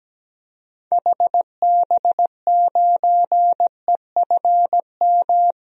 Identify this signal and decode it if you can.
Morse code, heard as a keyed tone, about 17 wpm, 700 Hz. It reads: HB9EFM
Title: DXCC-100 Call Signs - 6 Characters - Repeated 2x - 17wpm